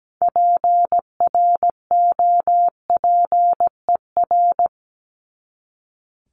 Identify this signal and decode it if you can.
Morse code, heard as a keyed tone, about 17 wpm, 700 Hz. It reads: PROPER
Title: LY Suffix - Top 100 Words with Base Words 17wpm